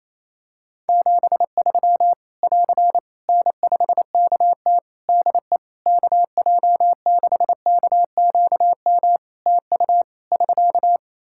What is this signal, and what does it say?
Morse code, heard as a keyed tone, about 27 wpm, 700 Hz. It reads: 73 <AR> N5KT DE KJ6KQM TU <SK>
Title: Practice QSO - Exchange 4 - Ending - 28wpm